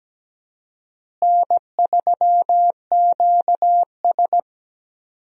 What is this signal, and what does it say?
Morse code, heard as a keyed tone, about 17 wpm, 700 Hz. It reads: N3QS
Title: USA Call Signs - 4 Characters - Repeated 3x - 17wpm